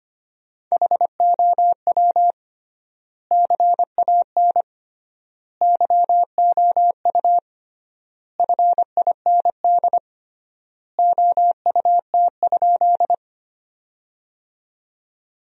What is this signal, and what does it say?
Morse code, heard as a keyed tone, about 25 wpm, 700 Hz. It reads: HOW CAN YOU FIND OUT?
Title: 3x Word-Spacing - Sentences from Top 200 Words - 25wpm